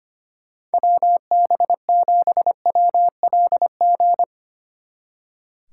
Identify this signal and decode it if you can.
Morse code, heard as a keyed tone, about 31 wpm, 700 Hz. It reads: WB7WLG
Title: Speed Racing - USA Call Signs 25wpm